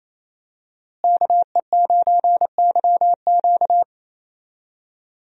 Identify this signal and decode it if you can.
Morse code, heard as a keyed tone, about 28 wpm, 700 Hz. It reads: KE9YQ
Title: USA Call Signs - 5 Characters - Repeated 2x - 28wpm